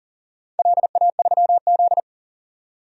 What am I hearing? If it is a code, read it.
Morse code, heard as a keyed tone, about 40 wpm, 700 Hz. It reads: LA3Z
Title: DXCC-100 Call Signs - 4 Characters - 40wpm